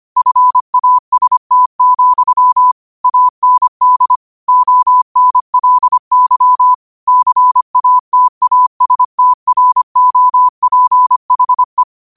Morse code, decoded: FAST, AND ONLY CATASTROPHE

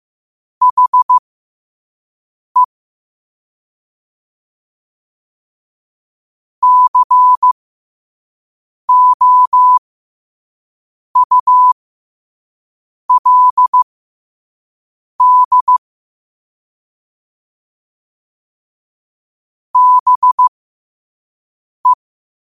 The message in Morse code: HE COULD BE

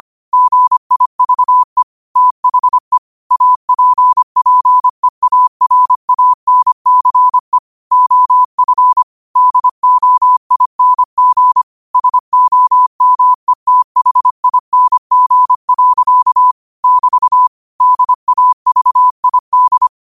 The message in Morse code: GIVE THE APPEARANCE OF DOING SOMETHING. <BT> DAVID

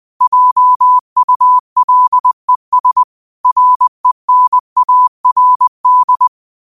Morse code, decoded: JULES RENARD

